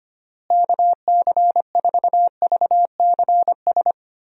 Morse code, heard as a keyed tone, about 25 wpm, 700 Hz. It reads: KC4VCS
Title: USA Call Signs - 6 Characters - Repeated 2x - 25wpm